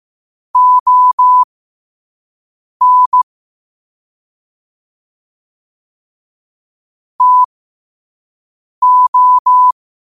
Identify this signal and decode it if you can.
Morse code, heard as a keyed tone, about 15 wpm, spaced as for 5 wpm, 1000 Hz. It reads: ON TO
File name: qotdfiles / 5wpm